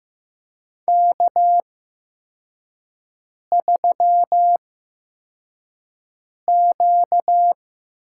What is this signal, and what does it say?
Morse code, heard as a keyed tone, about 15 wpm, spaced as for 4 wpm, 700 Hz. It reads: K3Q
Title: USA Call Signs - 8x Character Spacing - 15wpm